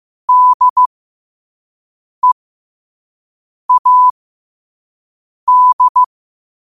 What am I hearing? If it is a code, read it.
Morse code, heard as a keyed tone, about 15 wpm, spaced as for 5 wpm, 1000 Hz. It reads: DEAD
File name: qotdfiles / 5wpm